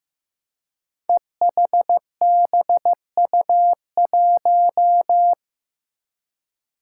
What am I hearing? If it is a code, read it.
Morse code, heard as a keyed tone, about 15 wpm, 700 Hz. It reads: EHBU1
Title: Five Challenging Letter-Number Combinations 15wpm